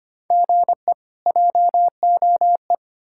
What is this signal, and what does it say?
Morse code, heard as a keyed tone, about 33 wpm, 700 Hz. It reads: GE JOE